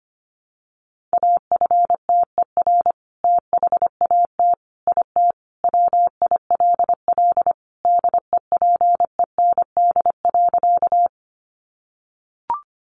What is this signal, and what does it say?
Morse code, heard as a keyed tone, about 25 wpm, 700 Hz. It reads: AFTER THAT IT WILL DEPEND.
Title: Sentences from Top 1300 Words 25wpm